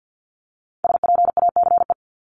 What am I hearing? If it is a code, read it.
Morse code, heard as a keyed tone, about 50 wpm, 700 Hz. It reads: SPACE
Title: Multi-letter Phonemes Repeated 3x 50wpm